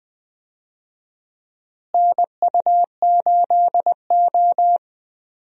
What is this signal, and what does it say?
Morse code, heard as a keyed tone, about 20 wpm, 700 Hz. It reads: NU8O
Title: USA Call Signs - 4 Characters - Repeated 3x - 20wpm